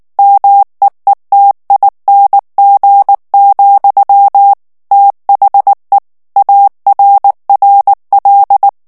Morse code, heard as a keyed tone, about 19 wpm, 800 Hz. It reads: MEETING, THE ARRL
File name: ANGLAIS texte clair 3 vitesse 18 mots minute